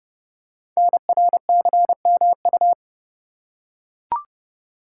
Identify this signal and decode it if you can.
Morse code, heard as a keyed tone, about 28 wpm, 700 Hz. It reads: NRCMU
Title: Code Groups - 5 Letters - 30wpm